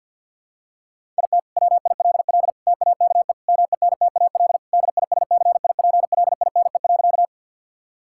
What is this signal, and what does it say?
Morse code, heard as a keyed tone, about 50 wpm, 700 Hz. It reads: IT WILL TAKE MENTAL DISCIPLINE.
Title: Sentences from Top 1000 Words - Review Part 6 - 50wpm